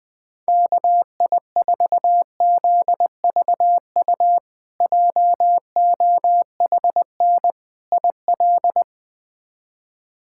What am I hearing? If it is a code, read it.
Morse code, heard as a keyed tone, about 20 wpm, 700 Hz. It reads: KI4ZVU JOHN IL
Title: K1USN SST QSO Practice - 20wpm